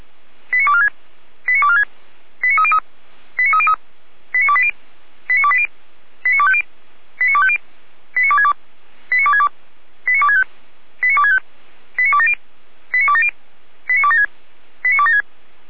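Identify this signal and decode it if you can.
Five-tone selective call (ZVEI), heard as quick runs of five tones, tones of about 70 ms. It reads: 89246 89246 89303 89303 89280 89280 89250 89250 89262 89262 89265 89265 89270 89270 89276 89276
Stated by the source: ZVEI five-tone call received by SDR, from the server of the Orlamünde fire brigade (feuerwehr-orlamuende).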